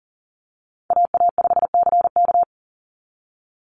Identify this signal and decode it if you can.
Morse code, heard as a keyed tone, about 38 wpm, 700 Hz. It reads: AA5CK
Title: NA SOTA Chaser Callsigns 40wpm